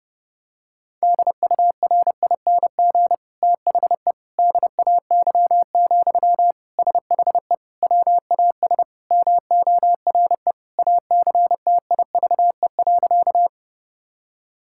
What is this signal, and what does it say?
Morse code, heard as a keyed tone, about 30 wpm, 700 Hz. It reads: DURING THE DAY, SHE WAS MORE ACTIVE.
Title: Sentences from Top 1000 Words - Review Part 6 - 30wpm